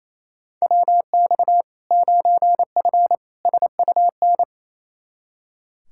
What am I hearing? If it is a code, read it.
Morse code, heard as a keyed tone, about 28 wpm, 700 Hz. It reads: WX 9F SUN
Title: Practice QSO - Exchange 2 - Info - 28wpm